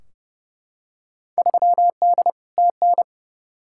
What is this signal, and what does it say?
Morse code, heard as a keyed tone, about 32 wpm, 700 Hz. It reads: 3D TN
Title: Field Day Exchanges Encore - 30wpm